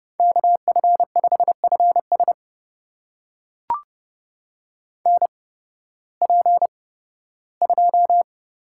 Morse code, heard as a keyed tone, about 29 wpm, spaced as for 16 wpm, 700 Hz. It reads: KF5FS NP2
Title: USA Call Signs - 8x Character Spacing - 30wpm